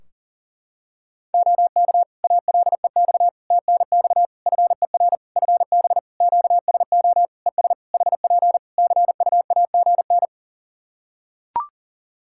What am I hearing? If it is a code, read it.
Morse code, heard as a keyed tone, about 37 wpm, 700 Hz. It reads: OK ALEX TNX FER FB QSO ES HP CUAGN
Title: Practice QSO - Exchange 4 - Ending - 40wpm